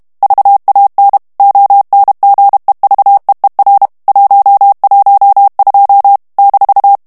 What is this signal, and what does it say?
Morse code, heard as a keyed tone, about 32 wpm, 800 Hz. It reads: VAN ONGEVEER 112 -
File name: morse mp3 / HOLLANDAIS texte clair 1 vitesse 30 mots minute